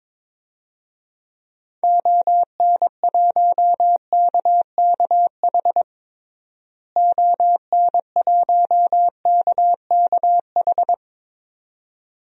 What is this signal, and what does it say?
Morse code, heard as a keyed tone, about 22 wpm, 700 Hz. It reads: ON1KKH ON1KKH
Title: DXCC-100 Call Signs - 6 Characters - Repeated 2x - 22wpm